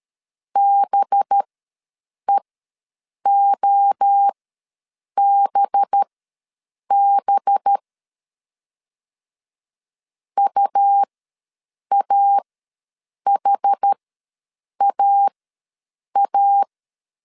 Morse code, heard as a keyed tone, about 13 wpm, spaced as for 7 wpm, 800 Hz. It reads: BEOBB UAHAA
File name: LEZIONI / 5 LBVC / 5 LB ws 12wpm 3034